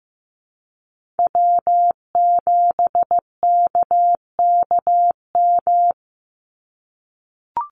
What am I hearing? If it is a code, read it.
Morse code, heard as a keyed tone, about 15 wpm, 700 Hz. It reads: W7KKM
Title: NA SOTA Chaser Callsigns 15wpm